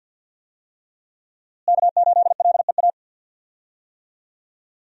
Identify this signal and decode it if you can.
Morse code, heard as a keyed tone, about 53 wpm, 700 Hz. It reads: K8LEA
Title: USA Call Signs - 5 Characters - Repeated 3x - 50wpm